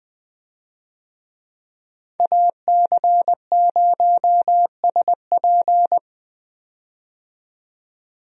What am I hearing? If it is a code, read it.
Morse code, heard as a keyed tone, about 20 wpm, 700 Hz. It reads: AC0SP
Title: USA Call Signs - 5 Characters - 20wpm